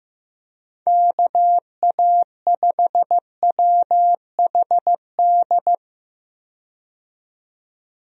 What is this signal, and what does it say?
Morse code, heard as a keyed tone, about 15 wpm, 700 Hz. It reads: KA5WHD
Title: USA Call Signs - 6 Characters - 15wpm